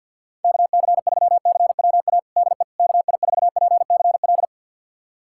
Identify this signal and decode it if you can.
Morse code, heard as a keyed tone, about 52 wpm, 700 Hz. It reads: KC3CWA DE KI4PCL
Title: USA Callsign DE Callsign 50wpm